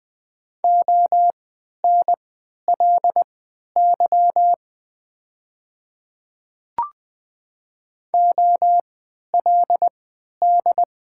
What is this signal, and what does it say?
Morse code, heard as a keyed tone, about 20 wpm, spaced as for 11 wpm, 700 Hz. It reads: ONLY OLD
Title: 3x Character Spacing - Top 200 Words - 20wpm